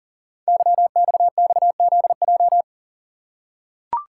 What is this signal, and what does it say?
Morse code, heard as a keyed tone, about 42 wpm, 690 Hz. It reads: YXXZJ
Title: Code Groups - 5 Letters - 40wpm